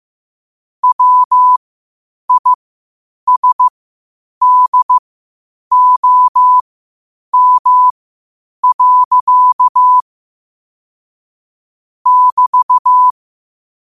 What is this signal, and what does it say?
Morse code, heard as a keyed tone, about 15 wpm, spaced as for 8 wpm, 1000 Hz. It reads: WISDOM. <BT>
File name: qotdfiles / 8wpm